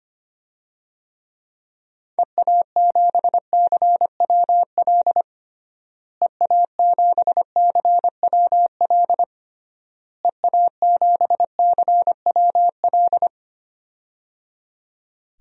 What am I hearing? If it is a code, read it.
Morse code, heard as a keyed tone, about 25 wpm, 700 Hz. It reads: EA7CWL EA7CWL EA7CWL
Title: DXCC-100 Call Signs - 6 Characters - Repeated 3x - 25wpm